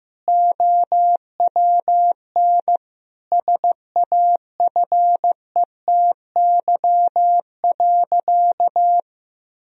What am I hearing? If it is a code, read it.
Morse code, heard as a keyed tone, about 15 wpm, 700 Hz. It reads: OWN SAFETY.
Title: Sentences from Top 1000 Words - Review Part 4 - 15wpm